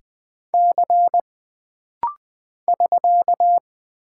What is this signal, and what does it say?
Morse code, heard as a keyed tone, about 20 wpm, 700 Hz. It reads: C <SK>
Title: Single Character (Letter:Number:Punctuation:Prosign) - Rapid-Fire - 20wpm